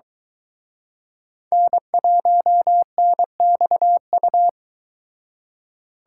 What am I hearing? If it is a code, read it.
Morse code, heard as a keyed tone, about 23 wpm, 700 Hz. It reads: N1NXU